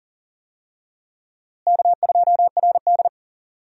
Morse code, heard as a keyed tone, about 40 wpm, 700 Hz. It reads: K2RD
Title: USA Call Signs - 4 Characters - Repeated 2x - 40wpm